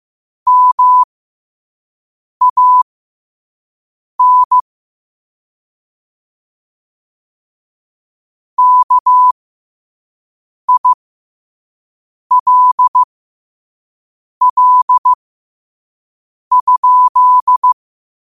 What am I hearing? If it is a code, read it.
Morse code, heard as a keyed tone, about 15 wpm, spaced as for 5 wpm, 1000 Hz. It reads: MAN KILL?
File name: 5wpm